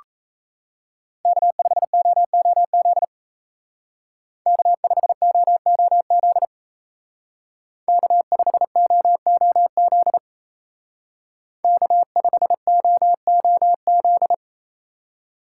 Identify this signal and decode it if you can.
Morse code, heard as a keyed tone, about 34 wpm, 700 Hz. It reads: K5OOZ K5OOZ K5OOZ K5OOZ
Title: Speed Racing - USA 1x3 Call Signs 28wpm